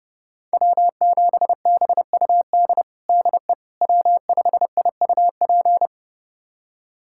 Morse code, heard as a keyed tone, about 30 wpm, 700 Hz. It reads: W7BUD DE W5IUP
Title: USA Callsign DE Callsign - 5 Characters - 30wpm